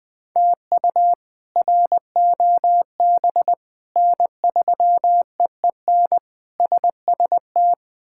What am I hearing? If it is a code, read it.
Morse code, heard as a keyed tone, about 20 wpm, 700 Hz. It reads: TU ROB N3EEN SST
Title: K1USN SST QSO Practice - 20wpm